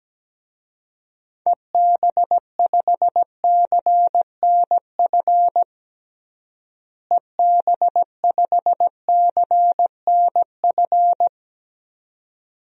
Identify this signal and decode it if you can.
Morse code, heard as a keyed tone, about 17 wpm, 700 Hz. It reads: EB5CNF EB5CNF
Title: DXCC-100 Call Signs - 6 Characters - Repeated 2x - 17wpm